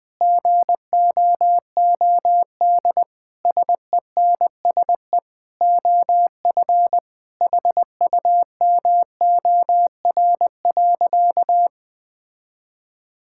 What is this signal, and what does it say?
Morse code, heard as a keyed tone, about 21 wpm, 700 Hz. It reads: GOOD SENSE OF HUMOR.
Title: Sentences with 2-Word Phrases - 20wpm